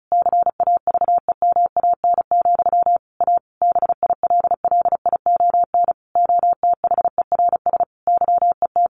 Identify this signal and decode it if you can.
Morse code, heard as a keyed tone, about 34 wpm, 700 Hz. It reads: CAVEMAN, A BILLION OTHERS YET
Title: A Sound Of Thunder 35wpm